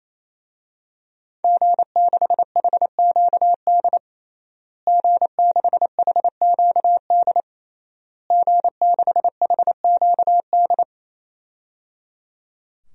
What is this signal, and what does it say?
Morse code, heard as a keyed tone, about 28 wpm, 700 Hz. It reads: G6HQD G6HQD G6HQD
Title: DXCC-100 Call Signs - 5 Characters - Repeated 3x - 28wpm